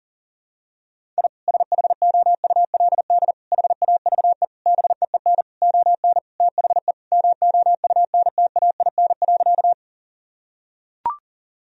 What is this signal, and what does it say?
Morse code, heard as a keyed tone, about 37 wpm, 700 Hz. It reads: I SHOULD HAVE BEEN ON THE MOUNTAIN.
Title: Sentences from Top 1000 Words - Review Part 1 - 40wpm